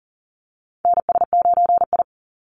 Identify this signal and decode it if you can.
Morse code, heard as a keyed tone, about 38 wpm, 700 Hz. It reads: NS9I
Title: Call Signs - 40wpm